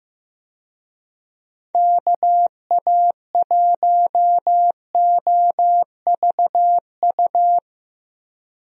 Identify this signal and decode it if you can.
Morse code, heard as a keyed tone, about 15 wpm, 700 Hz. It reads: KA1OVU
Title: USA Call Signs - 6 Characters - Repeated 3x - 15wpm